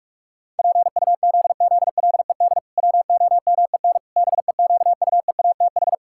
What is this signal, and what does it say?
Morse code, heard as a keyed tone, about 46 wpm, 700 Hz. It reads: PUZZLED WOMEN BEQUEATH
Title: Pangram Challenge v2 - 45wpm